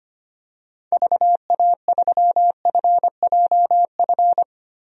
Morse code, heard as a keyed tone, about 25 wpm, 700 Hz. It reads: VA3FJF
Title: DXCC-100 Call Signs - 6 Characters - 25wpm